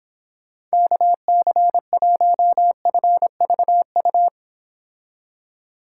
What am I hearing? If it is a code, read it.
Morse code, heard as a keyed tone, about 29 wpm, 700 Hz. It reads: KC1FVU